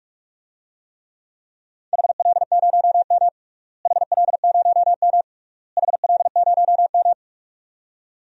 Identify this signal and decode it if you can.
Morse code, heard as a keyed tone, about 46 wpm, 700 Hz. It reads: HL0M HL0M HL0M
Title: DXCC-100 Call Signs - 4 Characters - Repeated 3x - 45wpm